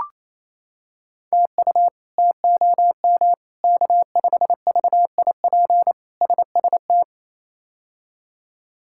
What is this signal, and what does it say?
Morse code, heard as a keyed tone, about 28 wpm, 700 Hz. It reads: TU TOM K5VIP SST
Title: K1USN SST QSO Practice - 28wpm